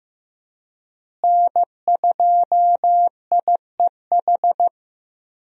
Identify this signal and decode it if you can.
Morse code, heard as a keyed tone, about 15 wpm, 700 Hz. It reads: N2IEH
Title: USA Call Signs - 5 Characters - Repeated 3x - 15wpm